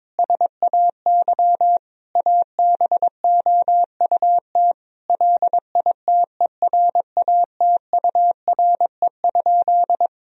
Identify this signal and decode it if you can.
Morse code, heard as a keyed tone, about 22 wpm, 700 Hz. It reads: SAY ABOUT LITERATURE?
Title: Sentences from Top 1000 Words - Review Part 3 - 22wpm